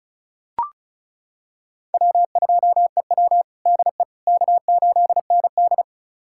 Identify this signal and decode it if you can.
Morse code, heard as a keyed tone, about 36 wpm, 700 Hz. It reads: W2EW DE K8ND
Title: USA Callsign DE Callsign - 4 Characters - 35wpm